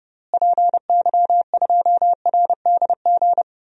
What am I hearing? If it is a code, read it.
Morse code, heard as a keyed tone, about 32 wpm, 700 Hz. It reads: PY2RDG
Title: DXCC-100 Call Signs - 6 Characters - Repeated 3x - 30wpm